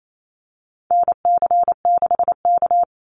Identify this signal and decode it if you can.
Morse code, heard as a keyed tone, about 28 wpm, 700 Hz. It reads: NC6K
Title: Call Signs - 28wpm